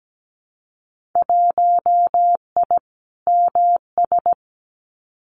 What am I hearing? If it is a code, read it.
Morse code, heard as a keyed tone, about 17 wpm, 700 Hz. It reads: 1I MS
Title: Winter Field Day Exchanges Encore 17wpm